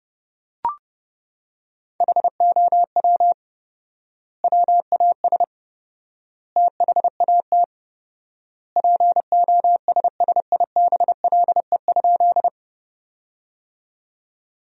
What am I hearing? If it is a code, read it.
Morse code, heard as a keyed tone, about 30 wpm, 700 Hz. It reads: HOW WAS THAT POSSIBLE?
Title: 4x Word-Spacing - Sentences from Top 500 Words - 30wpm